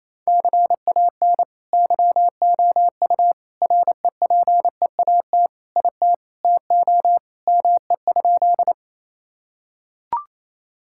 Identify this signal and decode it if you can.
Morse code, heard as a keyed tone, about 28 wpm, 700 Hz. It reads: CAN YOU REPEAT IT TO ME?
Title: Sentences from Top 1600 Words 28wpm